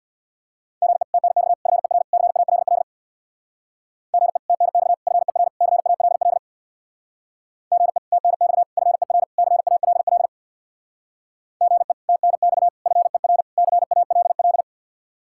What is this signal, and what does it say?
Morse code, heard as a keyed tone, about 63 wpm, 700 Hz. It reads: GE TNX FER CALL GE TNX FER CALL GE TNX FER CALL GE TNX FER CALL